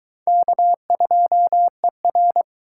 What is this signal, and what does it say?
Morse code, heard as a keyed tone, about 20 wpm, 700 Hz. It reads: K2ER